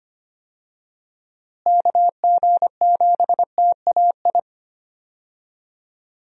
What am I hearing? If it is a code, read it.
Morse code, heard as a keyed tone, about 25 wpm, 700 Hz. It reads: KG7TAI
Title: USA Call Signs - 6 Characters - Repeated 3x - 25wpm